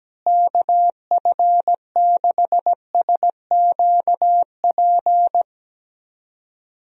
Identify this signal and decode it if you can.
Morse code, heard as a keyed tone, about 17 wpm, 700 Hz. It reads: KF6SQP
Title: USA Call Signs - 6 Characters - Repeated 2x - 17wpm